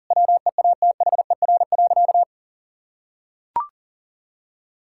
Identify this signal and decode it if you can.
Morse code, heard as a keyed tone, about 40 wpm, 700 Hz. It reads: WEATHER.
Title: Sentences from Top 1000 Words - Review Part 5 - 40wpm